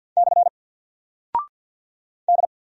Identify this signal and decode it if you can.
Morse code, heard as a keyed tone, about 48 wpm, 700 Hz. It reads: / D
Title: Single Character (Letter:Number:Punctuation:Prosign) 50wpm